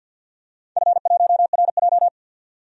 Morse code, heard as a keyed tone, about 48 wpm, 700 Hz. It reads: F1RJ
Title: DXCC-100 Call Signs - 4 Characters - 50wpm